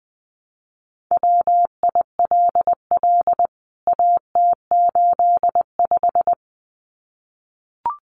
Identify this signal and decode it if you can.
Morse code, heard as a keyed tone, about 20 wpm, 700 Hz. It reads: WILL AT85
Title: CWT Exchange 20wpm